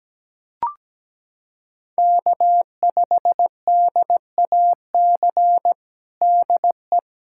Morse code, heard as a keyed tone, about 17 wpm, 700 Hz. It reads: K5DAC DE
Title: K1USN SST QSO Practice 17wpm